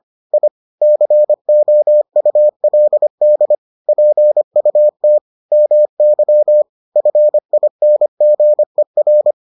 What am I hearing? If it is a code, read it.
Morse code, heard as a keyed tone, about 25 wpm, 600 Hz. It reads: I COULD PUT MY FINGER